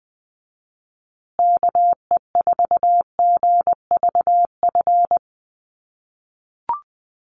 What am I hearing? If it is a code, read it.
Morse code, heard as a keyed tone, about 20 wpm, 700 Hz. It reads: KE4GVF
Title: USA Call Signs Repeated 3x 20wpm